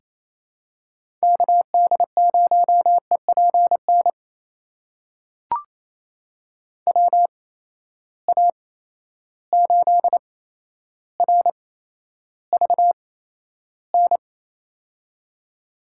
Morse code, heard as a keyed tone, about 28 wpm, spaced as for 12 wpm, 700 Hz. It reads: KD0EPN WA8RVN
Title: USA Call Signs - 6 Characters - 8x Character Spacing - 28wpm